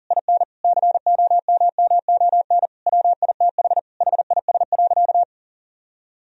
Morse code, heard as a keyed tone, about 40 wpm, 700 Hz. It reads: IN COMMON WITH HIS.
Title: Sentences from Top 1000 Words - Review Part 3 - 40wpm